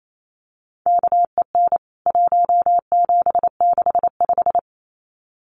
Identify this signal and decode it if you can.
Morse code, heard as a keyed tone, about 28 wpm, 700 Hz. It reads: KEN 1765
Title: CWT Exchange 28wpm